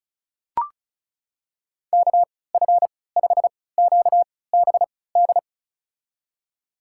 Morse code, heard as a keyed tone, about 36 wpm, spaced as for 20 wpm, 700 Hz. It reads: KF5QBD
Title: USA Call Signs - 6 Characters - 3x Character Spacing - 35wpm